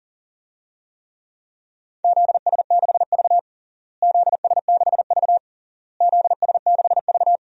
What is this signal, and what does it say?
Morse code, heard as a keyed tone, about 40 wpm, 700 Hz. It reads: ZS6V ZS6V ZS6V
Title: DXCC-100 Call Signs - 4 Characters - Repeated 3x - 40wpm